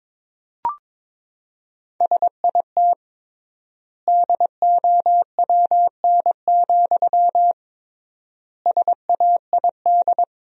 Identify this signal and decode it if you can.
Morse code, heard as a keyed tone, about 22 wpm, 700 Hz. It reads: SIT DOWN, SAID
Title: 3x Word-Spacing - Sentences from Top 500 Words - 22wpm